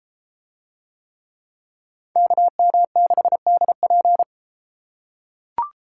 Morse code, heard as a keyed tone, about 32 wpm, 700 Hz. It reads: KM6DP